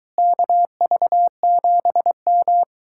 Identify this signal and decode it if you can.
Morse code, heard as a keyed tone, about 23 wpm, 700 Hz. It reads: KV7M